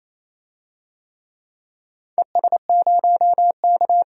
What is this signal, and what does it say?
Morse code, heard as a keyed tone, about 28 wpm, 700 Hz. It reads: ES0K
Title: DXCC-100 Call Signs - 4 Characters - 28wpm